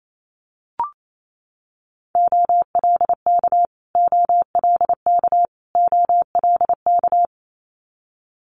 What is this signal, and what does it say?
Morse code, heard as a keyed tone, about 28 wpm, 700 Hz. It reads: OLK OLK OLK
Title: Multi-letter Phonemes Repeated 3x 28wpm